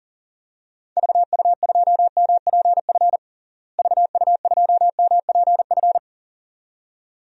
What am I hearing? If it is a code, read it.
Morse code, heard as a keyed tone, about 40 wpm, 700 Hz. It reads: VU2MPF VU2MPF
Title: DXCC-100 Call Signs - 6 Characters - Repeated 2x - 40wpm